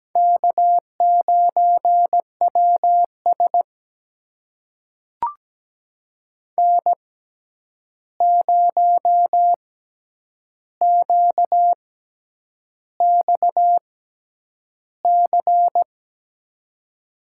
Morse code, heard as a keyed tone, about 17 wpm, spaced as for 8 wpm, 700 Hz. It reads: K9WS N0QXC
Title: USA Call Signs - 6x Character Spacing - 17wpm